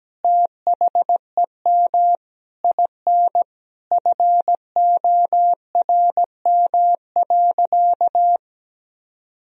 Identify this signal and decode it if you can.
Morse code, heard as a keyed tone, about 17 wpm, 700 Hz. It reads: THEM IN FORM.
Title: Sentences from Top 1000 Words - Review Part 1 - 17wpm